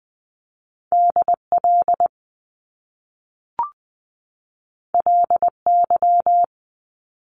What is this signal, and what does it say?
Morse code, heard as a keyed tone, about 20 wpm, 700 Hz. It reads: DL LY
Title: Top Two Letter Combinations in Words 20wpm